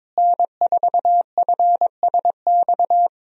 Morse code, heard as a keyed tone, about 22 wpm, 700 Hz. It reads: N4FSX
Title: USA Call Signs - 5 Characters - Repeated 3x - 22wpm